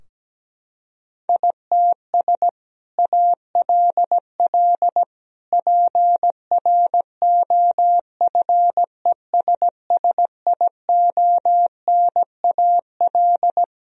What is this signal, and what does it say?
Morse code, heard as a keyed tone, about 17 wpm, 700 Hz. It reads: ITS ALL PROFESSIONAL
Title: Sentences from Top 1000 Words - Review Part 4 - 17wpm